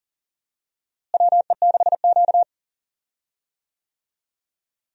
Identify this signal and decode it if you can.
Morse code, heard as a keyed tone, about 40 wpm, 700 Hz. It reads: WE6Q
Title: USA Call Signs - 4 Characters - Repeated 2x - 40wpm